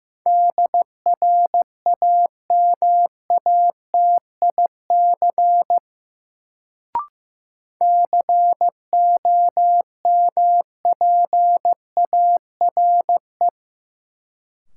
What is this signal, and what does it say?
Morse code, heard as a keyed tone, about 15 wpm, 700 Hz. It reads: DRAMATIC COMPARE
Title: Sentences from Top 1300 Words - New Word Drill Alt - 15wpm